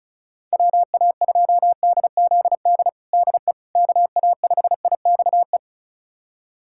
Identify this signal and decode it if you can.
Morse code, heard as a keyed tone, about 34 wpm, 700 Hz. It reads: WA2DZD DE KA5IXE
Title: USA Callsign DE Callsign - 6 Characters - 35wpm